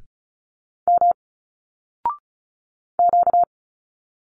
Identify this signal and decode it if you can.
Morse code, heard as a keyed tone, about 35 wpm, 700 Hz. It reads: M Q
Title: Single Letters 35wpm